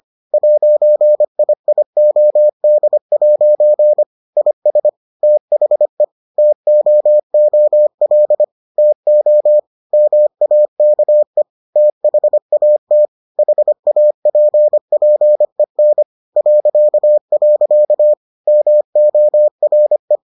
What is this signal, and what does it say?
Morse code, heard as a keyed tone, about 25 wpm, 600 Hz. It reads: 'IIOD' IS THE TOOL TO MAKE THAT HAPPEN .. MORE